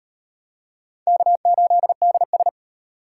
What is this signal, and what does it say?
Morse code, heard as a keyed tone, about 38 wpm, 700 Hz. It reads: K8DS